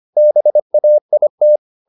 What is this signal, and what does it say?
Morse code, heard as a keyed tone, about 25 wpm, 600 Hz. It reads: BAIT
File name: the old man and the sea part 1 25wpm